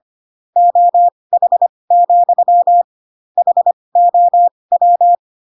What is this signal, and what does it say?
Morse code, heard as a keyed tone, about 25 wpm, spaced as for 20 wpm, 700 Hz. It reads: OH, HOW